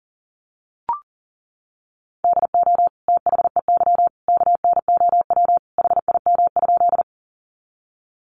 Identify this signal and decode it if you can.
Morse code, heard as a keyed tone, about 40 wpm, 700 Hz. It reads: DO THEY KNOW HIM?
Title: Sentences from Top 100 Words 40wpm